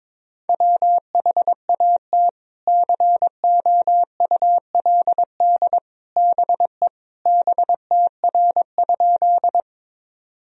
Morse code, heard as a keyed tone, about 22 wpm, 700 Hz. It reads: WHAT COULD BE BTR?
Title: Sentences from Top 400 Words and Abbreviations 22wpm